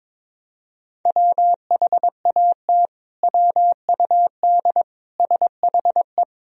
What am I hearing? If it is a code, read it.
Morse code, heard as a keyed tone, about 22 wpm, 700 Hz. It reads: WHAT WUD SHE
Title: Sentences from Top 100 Words and Abbreviations 22wpm